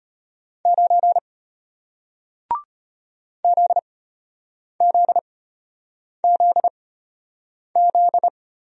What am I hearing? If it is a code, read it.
Morse code, heard as a keyed tone, about 34 wpm, spaced as for 8 wpm, 700 Hz. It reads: 9 ZZZZ